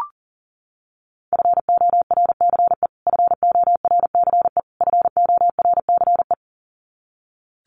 Morse code, heard as a keyed tone, about 40 wpm, 700 Hz. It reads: FORCE FORCE FORCE
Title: Top 500 Words Repeated 40wpm